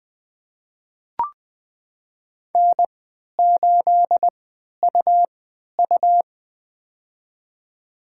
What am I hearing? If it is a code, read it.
Morse code, heard as a keyed tone, about 20 wpm, spaced as for 11 wpm, 700 Hz. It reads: N8UU
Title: USA Call Signs - 4 Characters - 3x Character Spacing - 20wpm